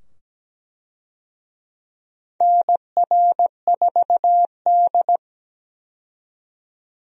Morse code, heard as a keyed tone, about 17 wpm, 700 Hz. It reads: NR4D
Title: USA Call Signs - 4 Characters - 17wpm